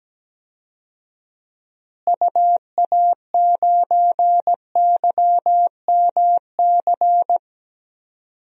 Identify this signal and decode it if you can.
Morse code, heard as a keyed tone, about 17 wpm, 700 Hz. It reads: UA9YMC
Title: DXCC-100 Call Signs - 6 Characters - Repeated 2x - 17wpm